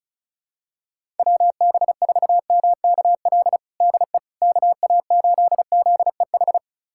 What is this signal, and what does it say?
Morse code, heard as a keyed tone, about 36 wpm, 700 Hz. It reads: WB4MKL DE KA8ZEH
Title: USA Callsign DE Callsign - 6 Characters - 35wpm